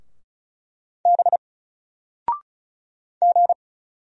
Morse code, heard as a keyed tone, about 35 wpm, 700 Hz. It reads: B G
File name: Single-Letter-Number-35wpm